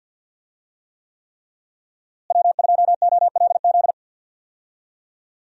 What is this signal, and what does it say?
Morse code, heard as a keyed tone, about 46 wpm, 700 Hz. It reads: W2OLZ
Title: USA Call Signs - 5 Characters - Repeated 3x - 50wpm